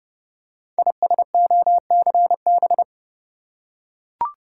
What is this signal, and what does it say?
Morse code, heard as a keyed tone, about 30 wpm, 700 Hz. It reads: ISOCB
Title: Code Groups - 5 Letters - 30wpm